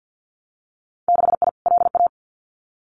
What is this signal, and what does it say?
Morse code, heard as a keyed tone, about 50 wpm, 700 Hz. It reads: 6I LA
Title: Winter Field Day Exchanges Encore 50wpm